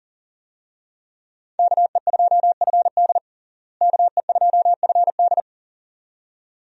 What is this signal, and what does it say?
Morse code, heard as a keyed tone, about 40 wpm, 700 Hz. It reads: KE2FD KE2FD
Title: USA Call Signs - 5 Characters - Repeated 2x - 40wpm